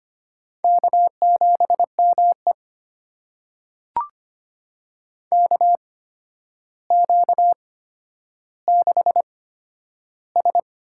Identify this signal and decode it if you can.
Morse code, heard as a keyed tone, about 25 wpm, spaced as for 11 wpm, 700 Hz. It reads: K7ME KQ6S
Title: USA Call Signs - 4 Characters - 8x Character Spacing - 25wpm